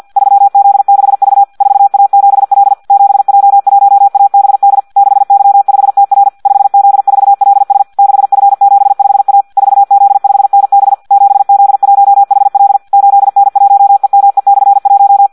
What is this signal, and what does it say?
Morse code, heard as a keyed tone, about 50 wpm, 800 Hz. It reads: 3Z6V 4A7L 7J1ABD 6Y5TR 5Z4LS 6F75A 4Z5NB 7Z1HL 8N1EME/1
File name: Audio practica CW / qrq-tx-50 2